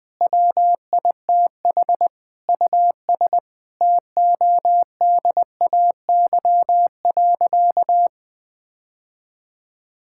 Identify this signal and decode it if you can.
Morse code, heard as a keyed tone, about 21 wpm, 700 Hz. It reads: WITH US TODAY.
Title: Sentences with 9-Word Phrases - 20wpm